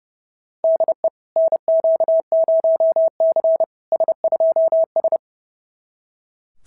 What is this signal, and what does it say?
Morse code, heard as a keyed tone, about 30 wpm, 690 Hz. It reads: DE NQ0C S2S
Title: SOTA QSO Practice 30wpm